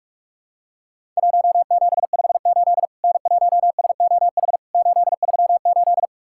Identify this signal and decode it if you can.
Morse code, heard as a keyed tone, about 46 wpm, 700 Hz. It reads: 1758 N1SOH 838
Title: Worked All Europe DX Contest - QTC Exchange - 45wpm